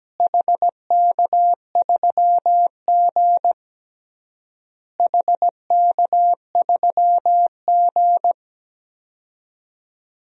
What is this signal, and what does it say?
Morse code, heard as a keyed tone, about 17 wpm, 700 Hz. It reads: HK3G HK3G
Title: DXCC-100 Call Signs - 4 Characters - Repeated 3x - 17wpm